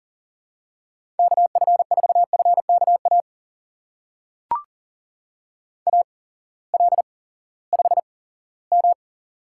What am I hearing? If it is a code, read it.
Morse code, heard as a keyed tone, about 39 wpm, spaced as for 20 wpm, 700 Hz. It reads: KF4FKA AL5M
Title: USA Call Signs - 8x Character Spacing - 40wpm